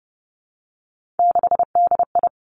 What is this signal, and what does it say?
Morse code, heard as a keyed tone, about 30 wpm, 700 Hz. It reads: 6DI
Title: Three Challenging Letter-Number Combinations 30wpm